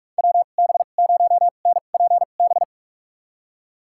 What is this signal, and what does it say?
Morse code, heard as a keyed tone, about 47 wpm, spaced as for 33 wpm, 700 Hz. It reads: WB0NPB
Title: USA Call Signs - 6 Characters - 2x Character Spacing - 45wpm